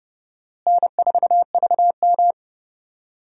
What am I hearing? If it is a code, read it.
Morse code, heard as a keyed tone, about 30 wpm, 700 Hz. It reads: N4VM
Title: USA Call Signs - 4 Characters - Repeated 3x - 30wpm